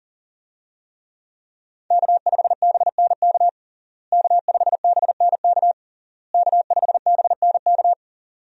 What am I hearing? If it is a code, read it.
Morse code, heard as a keyed tone, about 37 wpm, 700 Hz. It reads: K5BNK K5BNK K5BNK
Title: USA Call Signs - 5 Characters - Repeated 3x - 40wpm